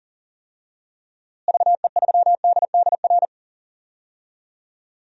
Morse code, heard as a keyed tone, about 40 wpm, 700 Hz. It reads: VE3DDR
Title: DXCC-100 Call Signs - 6 Characters - Repeated 3x - 40wpm